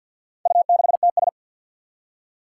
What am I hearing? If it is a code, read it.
Morse code, heard as a keyed tone, about 48 wpm, 700 Hz. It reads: U6TS
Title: Four Challenging Letter-Number Combinations 50wpm